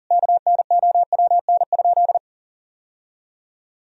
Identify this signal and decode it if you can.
Morse code, heard as a keyed tone, about 40 wpm, 700 Hz. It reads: KNOWN?
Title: Sentences from Top 1000 Words - Review Part 3 - 40wpm